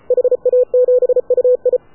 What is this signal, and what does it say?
Morse code, heard as a keyed tone, about 33 wpm, 500 Hz. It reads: HA7UI
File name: AUDIO2019 / HA7UI CW